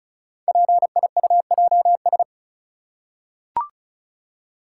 Morse code, heard as a keyed tone, about 36 wpm, 690 Hz. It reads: PIUJS
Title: Code Groups - 5 Letters - 35wpm